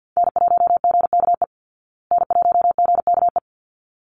Morse code, heard as a keyed tone, about 51 wpm, 700 Hz. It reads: N1GKE N1GKE
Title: Call Signs Repeated 3x - 50wpm